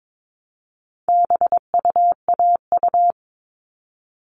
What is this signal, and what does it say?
Morse code, heard as a keyed tone, about 22 wpm, 700 Hz. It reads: BUAU
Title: All Cut Numbers - 4-Digit 22wpm